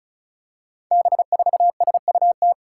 Morse code, heard as a keyed tone, about 36 wpm, 700 Hz. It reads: B4SUT
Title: Five Challenging Letter-Number Combinations 35wpm